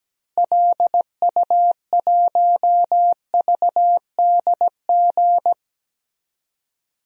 Morse code, heard as a keyed tone, about 17 wpm, 700 Hz. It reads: LU1VDG
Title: DXCC-100 Call Signs - 6 Characters - Repeated 2x - 17wpm